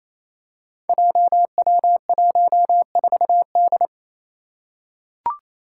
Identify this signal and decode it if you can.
Morse code, heard as a keyed tone, about 27 wpm, 700 Hz. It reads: JW14D
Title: Five Challenging Letter-Number Combinations 28wpm